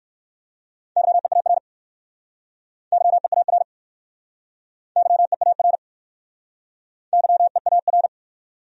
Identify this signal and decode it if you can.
Morse code, heard as a keyed tone, about 60 wpm, 700 Hz. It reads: YEAR YEAR YEAR YEAR